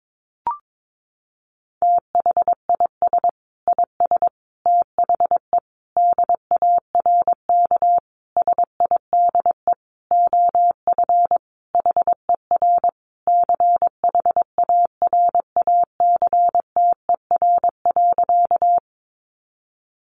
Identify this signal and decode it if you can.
Morse code, heard as a keyed tone, about 22 wpm, 700 Hz. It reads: THIS IS THE DARK SIDE OF HER CHARACTER.
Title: Sentences from Top 1200 Words 22wpm